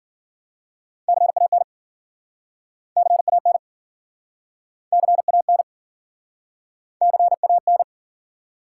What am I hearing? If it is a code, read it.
Morse code, heard as a keyed tone, about 55 wpm, 700 Hz. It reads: CAN CAN CAN CAN